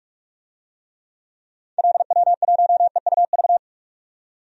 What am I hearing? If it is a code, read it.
Morse code, heard as a keyed tone, about 46 wpm, 700 Hz. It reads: LW1EUV
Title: DXCC-100 Call Signs - 6 Characters - 45wpm